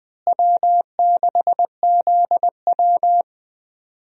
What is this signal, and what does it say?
Morse code, heard as a keyed tone, about 20 wpm, 700 Hz. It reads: W6ZW
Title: USA Call Signs - 4 Characters - Repeated 2x - 20wpm